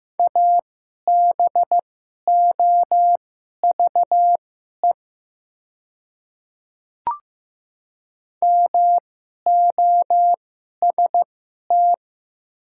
Morse code, heard as a keyed tone, about 15 wpm, spaced as for 11 wpm, 700 Hz. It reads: ABOVE MOST
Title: 2x Character Spacing - Top 500 Words - 15wpm